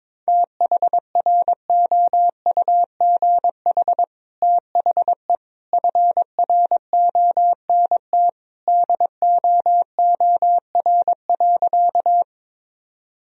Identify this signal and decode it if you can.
Morse code, heard as a keyed tone, about 22 wpm, 700 Hz. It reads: THROUGH THE FRONT DOOR.
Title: Sentences from Top 1000 Words - Review Part 2 - 22wpm